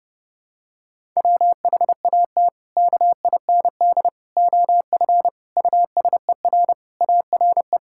Morse code, heard as a keyed tone, about 30 wpm, 700 Hz. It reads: WHAT KIND OF USER ARE
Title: Sentences from Top 1000 Words - Review Part 5 - 30wpm